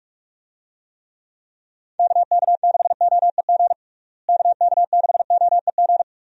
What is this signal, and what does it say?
Morse code, heard as a keyed tone, about 46 wpm, 700 Hz. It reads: KK6OEG KK6OEG
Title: USA Call Signs - 6 Characters - Repeated 3x - 45wpm